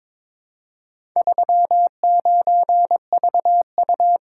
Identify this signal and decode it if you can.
Morse code, heard as a keyed tone, about 22 wpm, 700 Hz. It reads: 39VU
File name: practice / 22wpm / Four Letter-Number Combinations - 22wpm